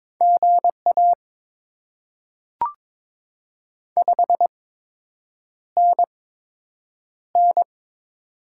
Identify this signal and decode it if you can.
Morse code, heard as a keyed tone, about 22 wpm, spaced as for 8 wpm, 700 Hz. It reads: GA 5NN
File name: Top 20 QSO Elements - 8x Character Spacing - 22wpm